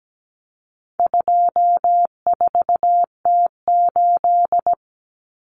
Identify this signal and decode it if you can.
Morse code, heard as a keyed tone, about 17 wpm, 700 Hz. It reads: 24T8
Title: Common Cut Numbers - 4-Digit 17wpm